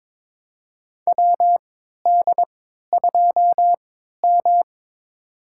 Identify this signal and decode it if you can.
Morse code, heard as a keyed tone, about 23 wpm, spaced as for 13 wpm, 700 Hz. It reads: WD2M
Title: USA Call Signs - 4 Characters - 3x Character Spacing - 22wpm